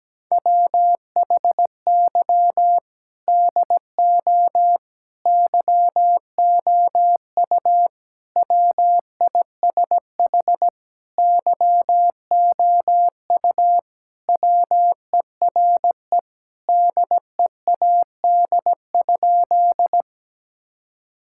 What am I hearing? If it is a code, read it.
Morse code, heard as a keyed tone, about 17 wpm, 700 Hz. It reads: WHY DO YOU WISH YOU WERE DEAD?
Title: Sentences from Top 1600 Words 17wpm